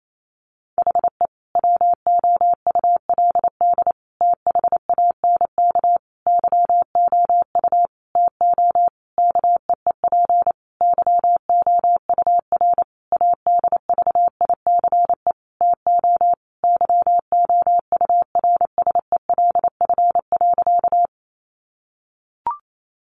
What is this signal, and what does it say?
Morse code, heard as a keyed tone, about 28 wpm, 700 Hz. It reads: HE WOULD THANK YOU TO KEEP YOUR ADVICE TO YOURSELF.
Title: Sentences from Top 1100 Words 28wpm